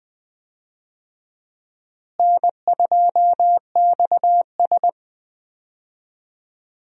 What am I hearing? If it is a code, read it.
Morse code, heard as a keyed tone, about 21 wpm, 700 Hz. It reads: N2XS
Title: USA Call Signs - 4 Characters - Repeated 2x - 20wpm